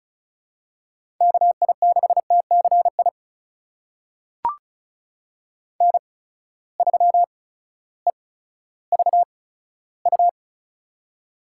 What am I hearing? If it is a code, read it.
Morse code, heard as a keyed tone, about 36 wpm, spaced as for 16 wpm, 700 Hz. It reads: KI6TCI N3EVU
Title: USA Call Signs - 8x Character Spacing - 35wpm